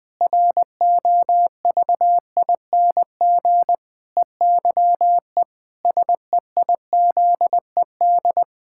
Morse code, heard as a keyed tone, about 20 wpm, 700 Hz. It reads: ROVING EYE SEIZED
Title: Pangram Challenge v2 - 20wpm